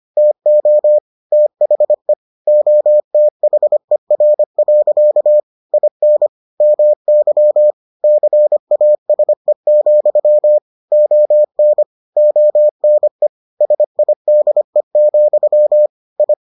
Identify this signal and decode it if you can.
Morse code, heard as a keyed tone, about 25 wpm, 600 Hz. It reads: TO THE OTHER. IN MY CASE, ON ONE SIDE, I